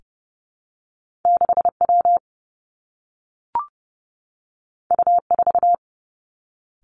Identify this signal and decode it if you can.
Morse code, heard as a keyed tone, about 31 wpm, 700 Hz. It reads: 6W U4
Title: Two Challenging Letter-Number Combinations 30wpm